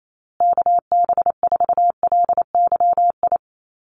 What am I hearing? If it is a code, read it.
Morse code, heard as a keyed tone, about 28 wpm, 700 Hz. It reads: KB4LYI
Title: USA Call Signs Repeated 2x 28wpm